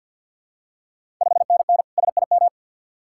Morse code, heard as a keyed tone, about 48 wpm, 690 Hz. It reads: 5NN SIM
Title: Ontario QSO Party Exchange 50wpm